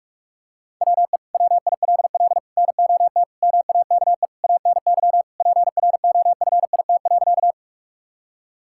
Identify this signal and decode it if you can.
Morse code, heard as a keyed tone, about 42 wpm, 700 Hz. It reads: WE WILL NOT MAKE ANY PROFIT.
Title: Sentences from Top 1000 Words - Review Part 5 - 45wpm